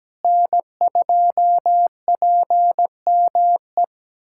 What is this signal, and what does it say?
Morse code, heard as a keyed tone, about 17 wpm, 700 Hz. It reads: N2PME
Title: USA Callsign DE Callsign - 5 Characters - 17wpm